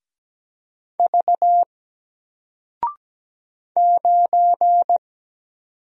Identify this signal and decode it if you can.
Morse code, heard as a keyed tone, about 17 wpm, 700 Hz. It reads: V 9
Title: Single Letter-Number 17wpm